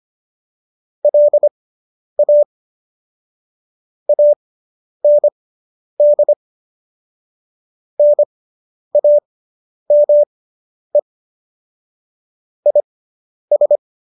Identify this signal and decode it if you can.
Morse code, heard as a keyed tone, about 25 wpm, spaced as for 10 wpm, 590 Hz. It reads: LA AND NAME IS